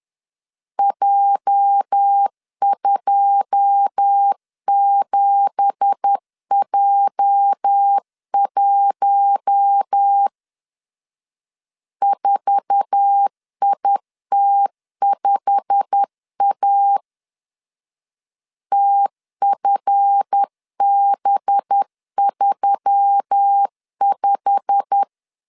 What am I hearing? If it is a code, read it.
Morse code, heard as a keyed tone, about 10 wpm, 800 Hz. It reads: J27J1 4IT5A TFB35
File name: CW / DET / 1 9 1500C S B 10wpm 4439